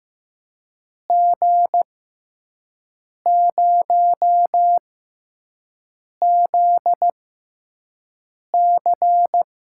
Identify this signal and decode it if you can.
Morse code, heard as a keyed tone, about 15 wpm, spaced as for 5 wpm, 700 Hz. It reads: G0ZC
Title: USA Call Signs - 5 Characters - 6x Character Spacing - 15wpm